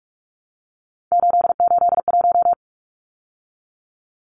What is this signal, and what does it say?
Morse code, heard as a keyed tone, about 44 wpm, 700 Hz. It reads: 881
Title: Numbers - 3-Digit 45wpm